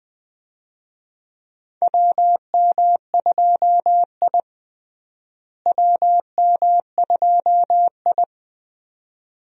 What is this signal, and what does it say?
Morse code, heard as a keyed tone, about 20 wpm, 700 Hz. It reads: WM2I WM2I
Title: USA Call Signs - 4 Characters - Repeated 2x - 20wpm